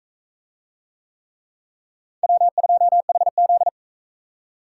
Morse code, heard as a keyed tone, about 42 wpm, 700 Hz. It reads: W2HZ